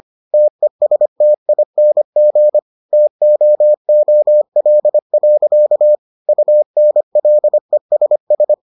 Morse code, heard as a keyed tone, about 25 wpm, 600 Hz. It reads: TESTING TOOL. UNLESS